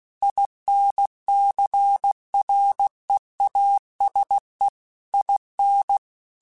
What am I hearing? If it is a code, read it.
Morse code, heard as a keyed tone, about 16 wpm, 800 Hz. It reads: INCREASE IN
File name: Technical Subjects / Technical Subjects-mp3s / XL @15